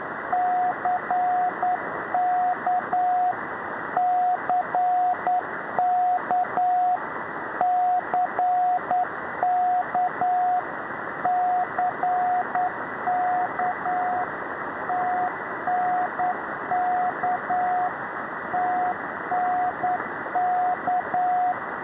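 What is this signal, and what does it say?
Morse code, heard as a keyed tone, about 9 wpm, 720 Hz, with radio noise. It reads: CK CK CK CK TNK TNK
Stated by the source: Morse numbers station.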